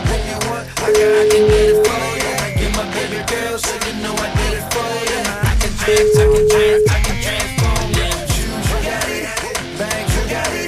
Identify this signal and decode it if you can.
Ringback tone (Europe, 425 Hz), with noise.